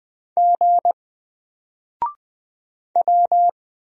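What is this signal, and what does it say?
Morse code, heard as a keyed tone, about 20 wpm, 700 Hz. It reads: G W
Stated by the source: Single letter and number Morse drill at 20 wpm, from N4LCD.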